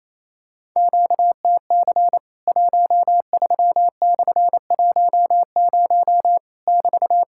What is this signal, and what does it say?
Morse code, heard as a keyed tone, about 28 wpm, 700 Hz. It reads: QTC 13/10 <BT>
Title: Worked All Europe DX Contest - QTC Exchange - 28wpm